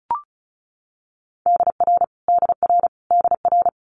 Morse code, heard as a keyed tone, about 36 wpm, 700 Hz. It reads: DR DR DR
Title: Multi-letter Phonemes Repeated 3x 35wpm